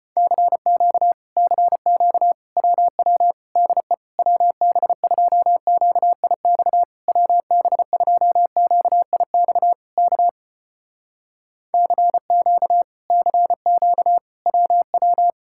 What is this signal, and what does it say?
Morse code, heard as a keyed tone, about 34 wpm, 700 Hz. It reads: CQ CQ WW DE WB2QIX WB2QIX K CQ CQ WW